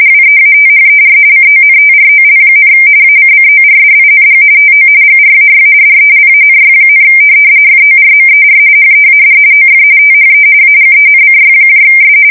RTTY, about 45.45 baud. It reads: WEATHER CONTROL, ACCORDING TO GRAY DELANY, FORMER HEAD OF THE AGENCYS
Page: 170 Shift 2125/2295